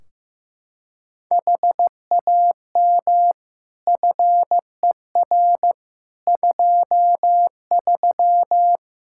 Morse code, heard as a keyed tone, about 16 wpm, 700 Hz. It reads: HAM FER 23
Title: Practice QSO - Exchange 3 - More Info - 15wpm